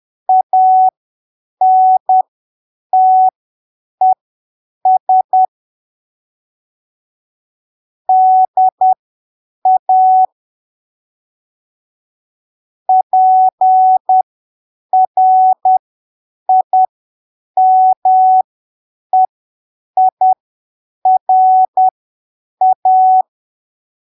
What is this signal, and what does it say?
Morse code, heard as a keyed tone, about 10 wpm, spaced as for 7 wpm, 750 Hz. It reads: ANTES DA PRIMEIRA
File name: arquivos / texto / texto 14 10ppm